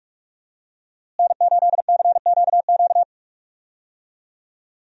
Morse code, heard as a keyed tone, about 44 wpm, 700 Hz. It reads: N8CQQ
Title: USA Call Signs - 5 Characters - Repeated 3x - 45wpm